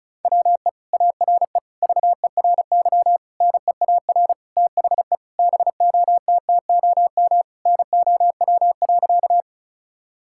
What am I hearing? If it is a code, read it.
Morse code, heard as a keyed tone, about 36 wpm, 700 Hz. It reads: WE ARE VERY NEAR THE BOTTOM NOW.
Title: Sentences from Top 1000 Words - Review Part 6 - 35wpm